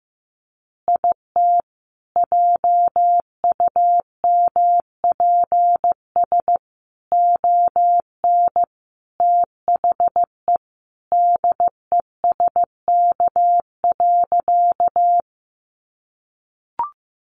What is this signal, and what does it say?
Morse code, heard as a keyed tone, about 15 wpm, 700 Hz. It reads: IT JUMPS ON THE DESK.
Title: Additional Easy Phrases 15wpm